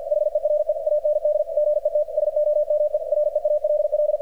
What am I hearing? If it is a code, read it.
Morse code, heard as a keyed tone, about 47 wpm, 600 Hz, with radio noise. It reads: HERE AND GET SOME REALL
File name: archived / CRSnet